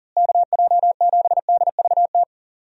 Morse code, heard as a keyed tone, about 40 wpm, 700 Hz. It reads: KJ7DVT
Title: USA Call Signs - 6 Characters - Repeated 3x - 40wpm